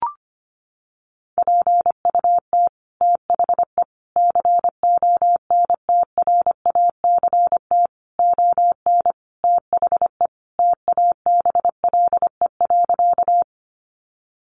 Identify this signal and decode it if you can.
Morse code, heard as a keyed tone, about 25 wpm, 700 Hz. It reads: PUT THE CONTRACT ON THE TABLE.
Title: Sentences from Top 1400 Words 25wpm